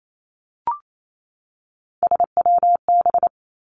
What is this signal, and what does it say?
Morse code, heard as a keyed tone, about 28 wpm, 700 Hz. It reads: SWB
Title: Three Challenging Letter-Number Combinations 28wpm